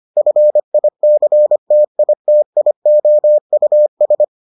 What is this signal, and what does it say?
Morse code, heard as a keyed tone, about 25 wpm, 600 Hz. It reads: FICTITIOUS